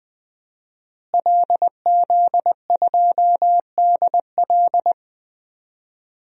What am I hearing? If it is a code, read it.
Morse code, heard as a keyed tone, about 20 wpm, 700 Hz. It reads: LZ2DL
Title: DXCC-100 Call Signs - 5 Characters - Repeated 3x - 20wpm